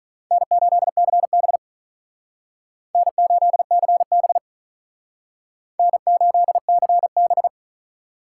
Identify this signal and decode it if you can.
Morse code, heard as a keyed tone, about 41 wpm, 700 Hz. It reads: N8CB N8CB N8CB
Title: Speed Racing - USA 1x2 Call Signs 35wpm